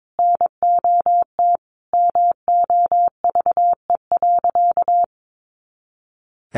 Morse code, heard as a keyed tone, about 22 wpm, 700 Hz. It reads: NOT MOVE.
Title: A Sound Of Thunder 22wpm